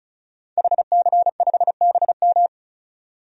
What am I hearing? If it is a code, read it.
Morse code, heard as a keyed tone, about 35 wpm, 700 Hz. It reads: HC5BM
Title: DXCC-100 Call Signs - 5 Characters - 35wpm